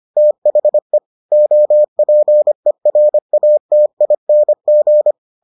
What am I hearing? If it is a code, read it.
Morse code, heard as a keyed tone, about 25 wpm, 600 Hz. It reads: THE OPERATING